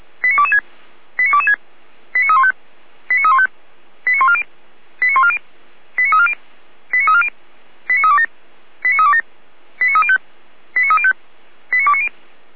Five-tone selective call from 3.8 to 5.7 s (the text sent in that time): 89140 89140